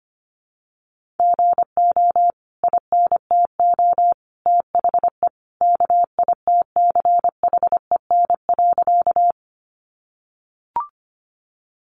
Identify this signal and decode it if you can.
Morse code, heard as a keyed tone, about 25 wpm, 700 Hz. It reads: GO INTO THE KITCHEN.
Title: Sentences from Top 1400 Words 25wpm